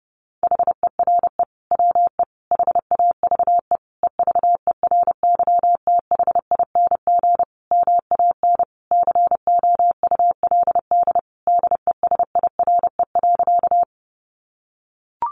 Morse code, heard as a keyed tone, about 30 wpm, 700 Hz. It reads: HERE WE HAVE EVERYTHING MAN COULD DESIRE.
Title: Sentences from Top 1100 Words 30wpm